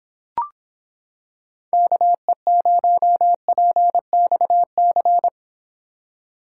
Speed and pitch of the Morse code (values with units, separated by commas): 26 wpm, 700 Hz